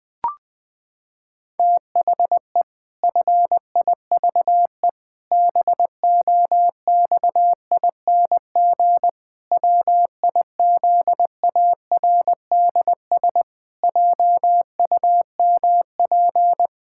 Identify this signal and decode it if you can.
Morse code, heard as a keyed tone, about 20 wpm, 700 Hz. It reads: THE FIVE BOXING WIZARDS JUMP
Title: Pangram Challenge v2 - 20wpm